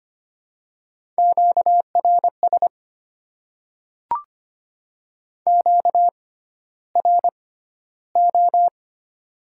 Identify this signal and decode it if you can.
Morse code, heard as a keyed tone, about 25 wpm, spaced as for 13 wpm, 700 Hz. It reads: QRS QRO
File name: Top 90 QSO Elements - 6x Character Spacing - 25wpm